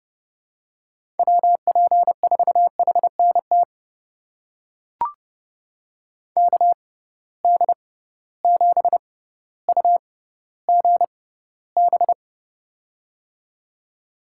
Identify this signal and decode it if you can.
Morse code, heard as a keyed tone, about 30 wpm, spaced as for 15 wpm, 700 Hz. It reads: WP4HNT KD7UGB
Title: USA Call Signs - 6 Characters - 6x Character Spacing - 30wpm